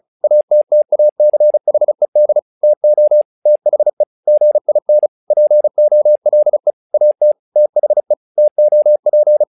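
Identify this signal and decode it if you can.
Morse code, heard as a keyed tone, about 35 wpm, 600 Hz. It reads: ATTACHED TO THE GIN POLE AT THE TOP
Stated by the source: Morse practice text "Tower Trouble" sent at 35 WPM, from the SKCC - Straight Key Century Club learning center audio files.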